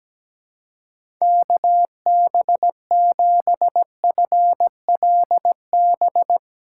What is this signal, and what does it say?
Morse code, heard as a keyed tone, about 17 wpm, 700 Hz. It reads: KB7FLB
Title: USA Call Signs - 3x Character Spacing - 17wpm